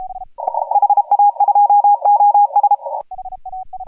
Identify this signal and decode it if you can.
Morse code, heard as a keyed tone, about 34 wpm, 820 Hz. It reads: HA2OS
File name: AUDIO2018 / HG1G CW